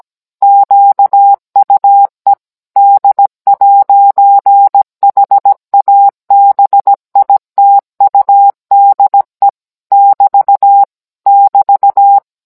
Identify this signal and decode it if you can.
Morse code, heard as a keyed tone, about 17 wpm, 800 Hz. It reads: QUE D'HABITUDE <BT> <BT>